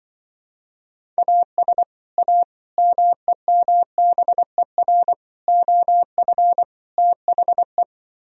Morse code, heard as a keyed tone, about 24 wpm, 700 Hz. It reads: AS A MEMBER OF THE
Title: Speed-Racing - Sets of 5 Words 20wpm